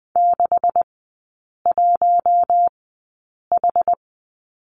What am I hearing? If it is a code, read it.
Morse code, heard as a keyed tone, about 20 wpm, spaced as for 8 wpm, 690 Hz. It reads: 61H
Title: 133 Lesson 22e - Question Mark - Three Characters 20wpm